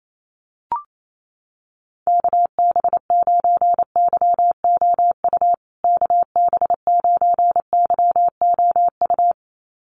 Morse code, heard as a keyed tone, about 28 wpm, 700 Hz. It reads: KB9YOU KB9YOU
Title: USA Call Signs Repeated 2x 28wpm